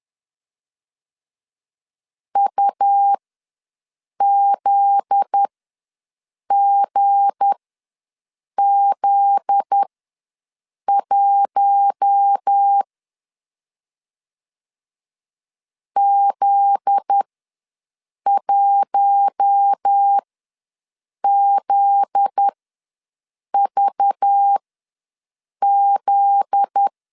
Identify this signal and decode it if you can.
Morse code, heard as a keyed tone, about 10 wpm, spaced as for 6 wpm, 800 Hz. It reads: UZGZ1 Z1ZVZ
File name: LEZIONI / 7 GF9Z / 7 GF9Z ws 10wpm 3919